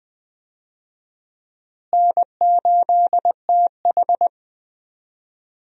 Morse code, heard as a keyed tone, about 21 wpm, 700 Hz. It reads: N8TH
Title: USA Call Signs - 4 Characters - Repeated 3x - 20wpm